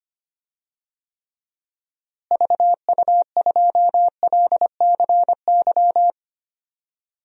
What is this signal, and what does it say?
Morse code, heard as a keyed tone, about 25 wpm, 700 Hz. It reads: VU2LCY
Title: DXCC-100 Call Signs - 6 Characters - Repeated 2x - 25wpm